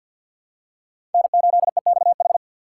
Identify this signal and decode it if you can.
Morse code, heard as a keyed tone, about 51 wpm, 690 Hz. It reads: N8EXH
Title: USA Call Signs - 5 Characters - 3x Character Spacing - 50wpm